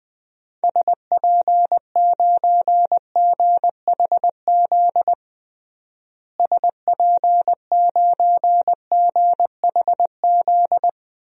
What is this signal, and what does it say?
Morse code, heard as a keyed tone, about 21 wpm, 700 Hz. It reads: SP9GHZ SP9GHZ
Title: DXCC-100 Call Signs - 6 Characters - Repeated 2x - 20wpm